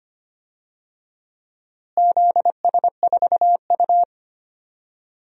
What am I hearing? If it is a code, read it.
Morse code, heard as a keyed tone, about 25 wpm, 700 Hz. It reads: ZS4U
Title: DXCC-100 Call Signs - 4 Characters - Repeated 2x - 25wpm